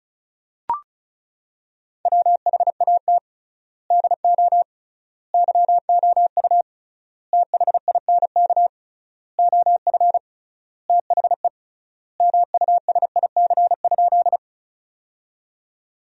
Morse code, heard as a keyed tone, about 36 wpm, 700 Hz. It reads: WHAT DO YOU THINK OF THE MUSIC?
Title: 3x Word-Spacing - Sentences from Top 300 Words - 35wpm